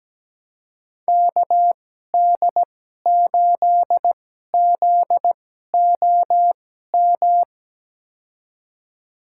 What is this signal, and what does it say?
Morse code, heard as a keyed tone, about 17 wpm, spaced as for 12 wpm, 700 Hz. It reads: KD8ZOM
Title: USA Call Signs - 6 Characters - 2x Character Spacing - 17wpm